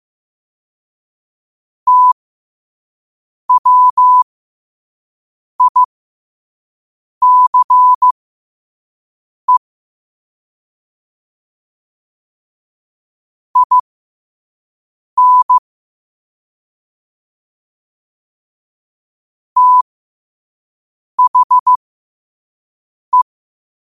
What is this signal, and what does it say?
Morse code, heard as a keyed tone, about 15 wpm, spaced as for 5 wpm, 1000 Hz. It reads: TWICE IN THE
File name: qotdfiles / 5wpm